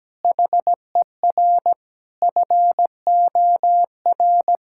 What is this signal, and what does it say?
Morse code, heard as a keyed tone, about 17 wpm, 700 Hz. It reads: HER FOR
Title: Sentences from Top 1600 Words 17wpm